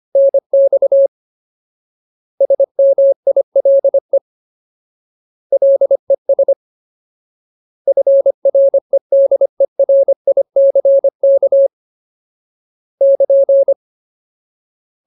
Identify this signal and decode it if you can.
Morse code, heard as a keyed tone, about 25 wpm, 550 Hz. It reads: NX SMILE LES FREDERICK <KN>
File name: Ultimate Word List / 25WPM Ultimate